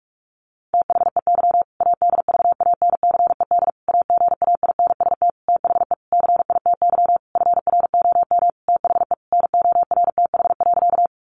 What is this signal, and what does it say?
Morse code, heard as a keyed tone, about 44 wpm, 700 Hz. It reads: THEY ADVANCED AGAINST THE CITY FROM THE NORTH.
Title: Sentences from Top 1200 Words 45wpm